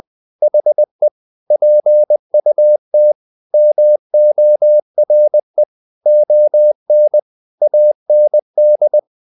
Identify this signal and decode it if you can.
Morse code, heard as a keyed tone, about 20 wpm, 600 Hz. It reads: HE PUT MORE ON AND
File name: the old man and the sea part 2 20wpm